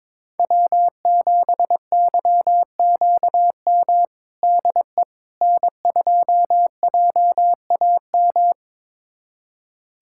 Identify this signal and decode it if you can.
Morse code, heard as a keyed tone, about 22 wpm, 700 Hz. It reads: W7YQM DE N2JAM
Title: USA Callsign DE Callsign - 5 Characters - 22wpm